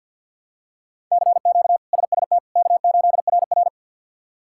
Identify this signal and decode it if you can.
Morse code, heard as a keyed tone, about 47 wpm, 700 Hz. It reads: CQ SST K8RR
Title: K1USN SST QSO Practice - 50wpm